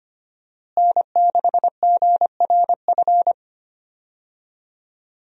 Morse code, heard as a keyed tone, about 25 wpm, 700 Hz. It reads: N6GRF
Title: USA Call Signs - 5 Characters - Repeated 3x - 25wpm